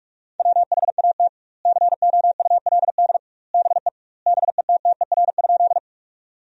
Morse code, heard as a keyed tone, about 46 wpm, 700 Hz. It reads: WHAT COULD BE BETTER?
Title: 2x Word-Spacing - Sentences from Top 400 Words - 45wpm